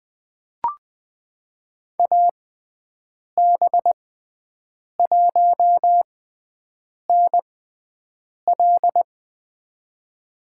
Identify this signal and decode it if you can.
Morse code, heard as a keyed tone, about 21 wpm, spaced as for 7 wpm, 700 Hz. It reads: AB1NL
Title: USA Call Signs - 6x Character Spacing - 20wpm